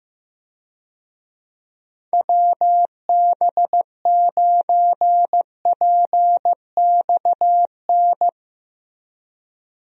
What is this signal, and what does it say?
Morse code, heard as a keyed tone, about 15 wpm, 700 Hz. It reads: WB9PXN
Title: USA Call Signs - 6 Characters - Repeated 3x - 15wpm